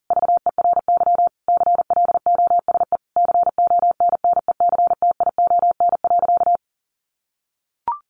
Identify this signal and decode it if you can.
Morse code, heard as a keyed tone, about 40 wpm, 700 Hz. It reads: VERY CLOSE CONNECTION.
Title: Sentences from Top 1300 Words 40wpm